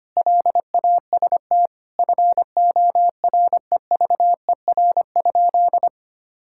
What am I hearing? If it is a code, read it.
Morse code, heard as a keyed tone, about 25 wpm, 700 Hz. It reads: LAST FOREVER?
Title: Sentences from Top 1600 Words 25wpm